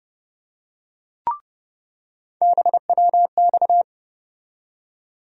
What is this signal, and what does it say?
Morse code, heard as a keyed tone, about 32 wpm, 700 Hz. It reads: BWX
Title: USA Call Sign Postfixes - 3 Letters 30wpm